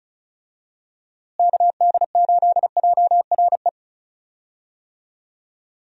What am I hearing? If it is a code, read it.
Morse code, heard as a keyed tone, about 36 wpm, 700 Hz. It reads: KD8JRE
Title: USA Call Signs - 6 Characters - Repeated 3x - 35wpm